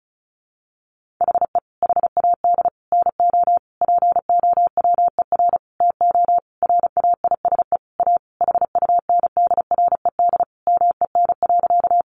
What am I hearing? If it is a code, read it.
Morse code, heard as a keyed tone, about 34 wpm, 700 Hz. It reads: HE HAD NO POWER TO RAISE A HUNDRED MEN.
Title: Sentences from Top 1400 Words 35wpm